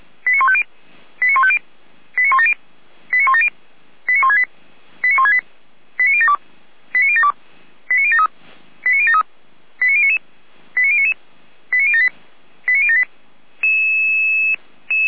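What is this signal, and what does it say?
Five-tone selective call (ZVEI), heard as quick runs of five tones, tones of about 70 ms. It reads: 89150 89150 89170 89170 89168 89168 89072 89072 89073 89073 89099 89099 89078 89078
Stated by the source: Received on SDR, ZVEI five-tone selective call.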